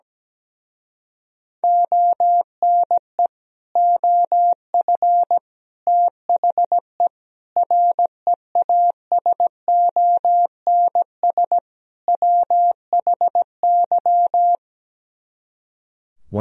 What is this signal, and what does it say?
Morse code, heard as a keyed tone, about 17 wpm, 700 Hz. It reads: ONE OF THE REASONS WHY
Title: Speed-Racing - Sets of 5 Words 17wpm